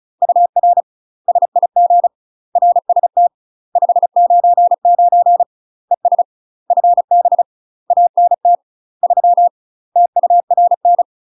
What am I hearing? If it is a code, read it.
Morse code, heard as a keyed tone, about 34 wpm, 700 Hz. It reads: UR SIG RST 599 ES FB ANT 3 TURN
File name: UK MASTER QSOs TRAINING / 35WPM / UK MASTER QSOs 0000